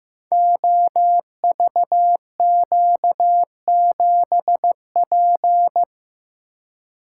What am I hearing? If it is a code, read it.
Morse code, heard as a keyed tone, about 15 wpm, 700 Hz. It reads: OVQ7P
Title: Code Groups - 5 Letter:Numbers - 15wpm